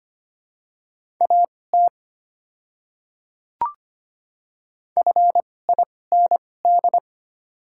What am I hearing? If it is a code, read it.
Morse code, heard as a keyed tone, about 25 wpm, spaced as for 18 wpm, 700 Hz. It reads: AT FIND
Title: 2x Character Spacing - Top 100 Words - 25wpm